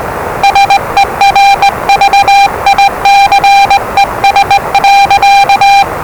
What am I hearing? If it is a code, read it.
Morse code, heard as a keyed tone, about 20 wpm, 800 Hz, with noise. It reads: SERVICES.
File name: WellingtonRadioZlwCloseDownOnHfAndMf / 01-Zlw 4277khz 417.5 qrt until 0925z